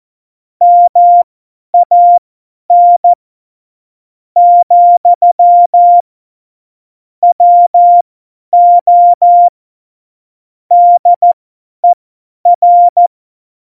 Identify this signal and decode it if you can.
Morse code, heard as a keyed tone, about 14 wpm, spaced as for 9 wpm, 700 Hz. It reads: MAN, WO DER
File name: Morsekurs / Witze / Witz 02. Oktober 2018, 50 F70